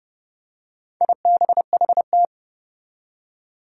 Morse code, heard as a keyed tone, about 30 wpm, 700 Hz. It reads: IBHT
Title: Four Challenging Letter-Number Combinations 30wpm